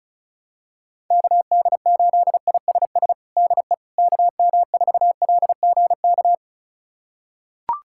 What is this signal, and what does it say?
Morse code, heard as a keyed tone, about 36 wpm, 700 Hz. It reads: KD8ISS DE KM4LGK
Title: USA Callsign DE Callsign - 6 Characters - 35wpm